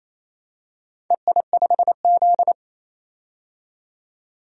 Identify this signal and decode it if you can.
Morse code, heard as a keyed tone, about 28 wpm, 700 Hz. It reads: EI5Z
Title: DXCC-100 Call Signs - 4 Characters - 28wpm